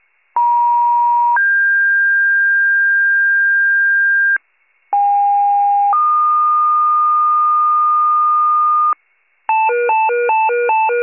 Two-tone page, heard as a one-second tone, then a three-second tone, about 950 then 1600 Hz, about 810 then 1220 Hz, with noise. It is heard from 0.4 to 8.9 s.